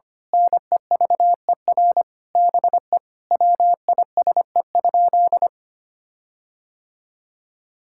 Morse code, heard as a keyed tone, about 25 wpm, 700 Hz. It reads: NEVER BE WISE?
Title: Sentences from Top 1600 Words 25wpm